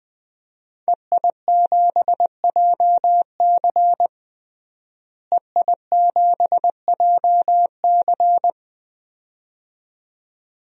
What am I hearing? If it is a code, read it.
Morse code, heard as a keyed tone, about 20 wpm, 700 Hz. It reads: EI7JC EI7JC
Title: DXCC-100 Call Signs - 5 Characters - Repeated 2x - 20wpm